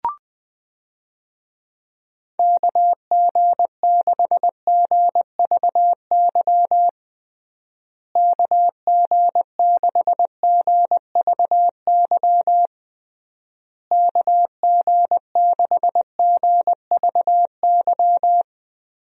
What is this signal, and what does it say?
Morse code, heard as a keyed tone, about 20 wpm, 700 Hz. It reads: KG6GVY KG6GVY KG6GVY
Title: USA Call Signs - 6 Characters - Repeated 3x - 20wpm